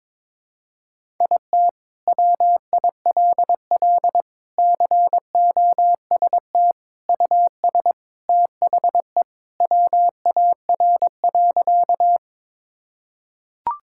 Morse code, heard as a keyed tone, about 22 wpm, 700 Hz. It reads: IT WILL COST US THE WAR.
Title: Sentences from Top 1000 Words - Review Part 5 - 22wpm